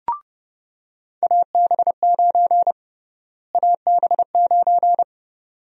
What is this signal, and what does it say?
Morse code, heard as a keyed tone, about 32 wpm, 700 Hz. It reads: AB9 AB9
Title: USA Call Sign Prefixes - Repeated 2x - 30wpm